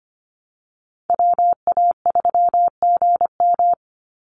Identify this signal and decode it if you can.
Morse code, heard as a keyed tone, about 25 wpm, 700 Hz. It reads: WA3GM
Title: Call Signs - 25wpm